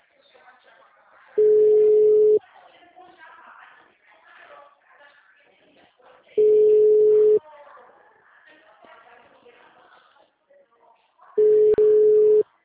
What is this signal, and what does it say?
Ringback tone (Europe, 425 Hz), with speech.